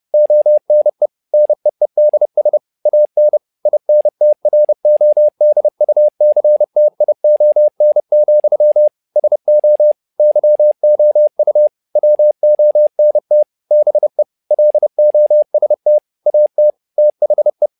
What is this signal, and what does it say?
Morse code, heard as a keyed tone, about 30 wpm, 600 Hz. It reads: ONE NEEDS AN INTRODUCTION, SO YOU WONT BE LOST AT THE
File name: learning center / tower trouble / tower trouble 30wpm